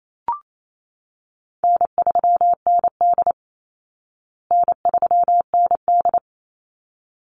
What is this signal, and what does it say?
Morse code, heard as a keyed tone, about 28 wpm, 700 Hz. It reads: N3ND N3ND
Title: Easy Call Signs Repeated 2x - 28wpm